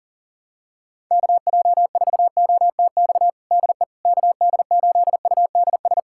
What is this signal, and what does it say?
Morse code, heard as a keyed tone, about 37 wpm, 700 Hz. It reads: KJ4OTX DE KD8UDS
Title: USA Callsign DE Callsign - 6 Characters - 40wpm